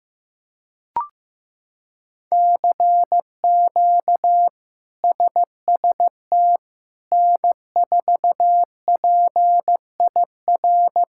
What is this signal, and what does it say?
Morse code, heard as a keyed tone, about 15 wpm, 700 Hz. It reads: CQ SST N4PIR
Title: K1USN SST QSO Practice - 15wpm